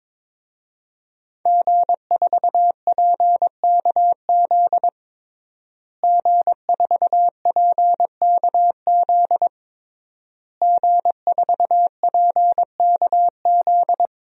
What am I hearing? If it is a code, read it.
Morse code, heard as a keyed tone, about 23 wpm, 700 Hz. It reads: G4PKZ G4PKZ G4PKZ
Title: DXCC-100 Call Signs - 5 Characters - Repeated 3x - 22wpm